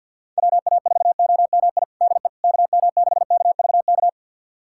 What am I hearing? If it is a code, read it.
Morse code, heard as a keyed tone, about 46 wpm, 700 Hz. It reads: WA4OMI DE KM6KVK
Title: USA Callsign DE Callsign - 6 Characters - 50wpm